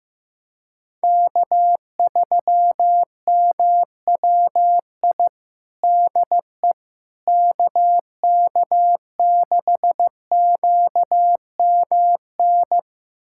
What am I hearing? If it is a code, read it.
Morse code, heard as a keyed tone, about 15 wpm, 700 Hz. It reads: K3MWI DE KK6QMN
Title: USA Callsign DE Callsign 15wpm